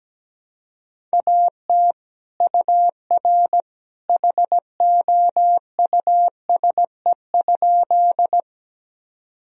Morse code, heard as a keyed tone, about 17 wpm, 700 Hz. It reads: AT UR HOUSE?
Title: Sentences from Top 200 Words and Abbreviations 17wpm